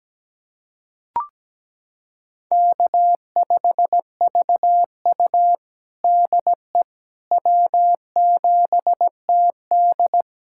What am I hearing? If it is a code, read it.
Morse code, heard as a keyed tone, about 17 wpm, 700 Hz. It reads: K5VU DE W7TD
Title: USA Callsign DE Callsign - 4 Characters - 17wpm